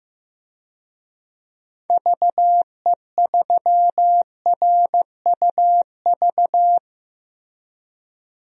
Morse code, heard as a keyed tone, about 15 wpm, 700 Hz. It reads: VE3RUV
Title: DXCC-100 Call Signs - 6 Characters - 15wpm